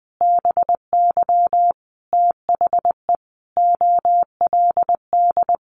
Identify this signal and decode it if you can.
Morse code, heard as a keyed tone, about 20 wpm, 700 Hz. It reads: BY THE OLD
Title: Sentences from Top 1200 Words 20wpm